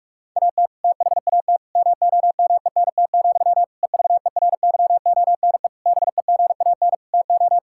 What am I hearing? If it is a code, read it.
Morse code, heard as a keyed tone, about 46 wpm, 700 Hz. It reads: AT THAT MOMENT, EVERYONE BEGAN TO
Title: Sentences from Top 1000 Words - Review Part 5 - 45wpm